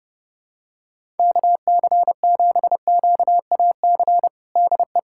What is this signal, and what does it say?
Morse code, heard as a keyed tone, about 32 wpm, 700 Hz. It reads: KC7QAC DE
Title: USA Callsign DE Callsign - 6 Characters - 30wpm